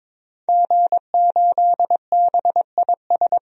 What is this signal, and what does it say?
Morse code, heard as a keyed tone, about 22 wpm, 700 Hz. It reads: G8BIS
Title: DXCC-100 Call Signs - 5 Characters - 22wpm